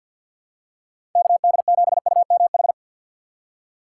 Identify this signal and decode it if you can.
Morse code, heard as a keyed tone, about 53 wpm, 700 Hz. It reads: KD7UMH
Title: USA Call Signs - 6 Characters - 8x Character Spacing - 50wpm